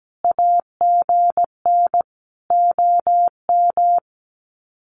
Morse code, heard as a keyed tone, about 17 wpm, 700 Hz. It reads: AGN OM
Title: Common QSO Phrases 17wpm